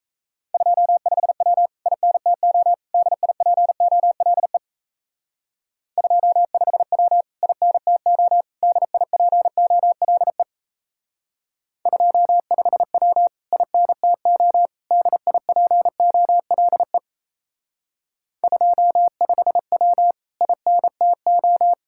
Morse code, heard as a keyed tone, about 34 wpm, 700 Hz. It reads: 25W INTO DIPOLE 25W INTO DIPOLE 25W INTO DIPOLE 25W INTO
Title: Speed-Racing Common QSO Phrases - 28wpm